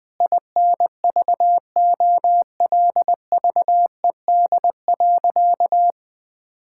Morse code, heard as a keyed tone, about 20 wpm, 700 Hz. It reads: INVOLVED.
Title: Sentences from Top 1000 Words - Review Part 5 - 20wpm